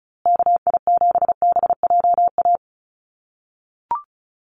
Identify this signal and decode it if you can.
Morse code, heard as a keyed tone, about 34 wpm, 700 Hz. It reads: KI7BJA
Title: USA Call Signs Repeated 3x 35wpm